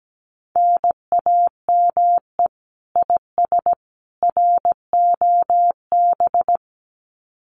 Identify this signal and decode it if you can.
Morse code, heard as a keyed tone, about 17 wpm, 700 Hz. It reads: NAME IS ROB
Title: Common QSO Phrases 17wpm